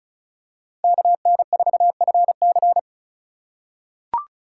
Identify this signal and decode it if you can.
Morse code, heard as a keyed tone, about 36 wpm, 700 Hz. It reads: KN4FC
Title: USA Call Signs - 2x Character Spacing - 35wpm